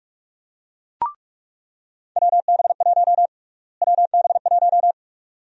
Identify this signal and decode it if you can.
Morse code, heard as a keyed tone, about 44 wpm, 700 Hz. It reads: WB1 WB1
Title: USA Call Sign Prefixes - Repeated 2x - 45wpm